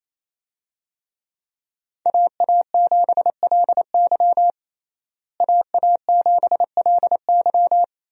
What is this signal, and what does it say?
Morse code, heard as a keyed tone, about 28 wpm, 700 Hz. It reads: AA7LY AA7LY
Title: USA Call Signs - 5 Characters - Repeated 3x - 28wpm